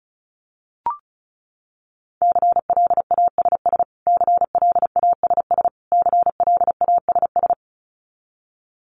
Morse code, heard as a keyed tone, about 34 wpm, 700 Hz. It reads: CLASS CLASS CLASS
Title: Top 500 Words Repeated 35wpm